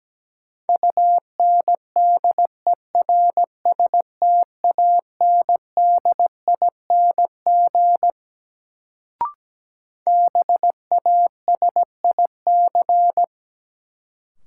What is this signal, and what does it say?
Morse code, heard as a keyed tone, about 17 wpm, 700 Hz. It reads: UNDERSTANDING BASIC
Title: Sentences from Top 1000 Words - Review Part 3 - New Word Drill - 17wpm